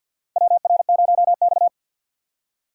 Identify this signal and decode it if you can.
Morse code, heard as a keyed tone, about 50 wpm, 690 Hz. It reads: WR0X
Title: USA Call Signs - 4 Characters - 6x Character Spacing - 50wpm